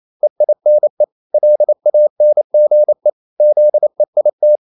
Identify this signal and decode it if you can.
Morse code, heard as a keyed tone, about 28 wpm, 600 Hz. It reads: EINE LANGE ZEIT
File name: Liebespaar4 28wpm